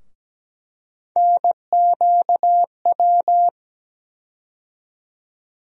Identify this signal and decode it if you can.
Morse code, heard as a keyed tone, about 17 wpm, 700 Hz. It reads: NQW
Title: USA Call Sign Postfixes - 3 Letters 17wpm